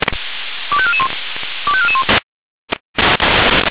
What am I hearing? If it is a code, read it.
Five-tone selective call (ZVEI), heard as quick runs of five tones, tones of about 70 ms. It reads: 25441 25441
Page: /496/SDR/ZVEI/25431/